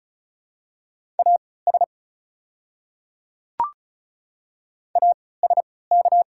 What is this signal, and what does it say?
Morse code, heard as a keyed tone, about 34 wpm, spaced as for 20 wpm, 700 Hz. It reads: AS ASK
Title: 3x Character Spacing - Top 300 Words - 35wpm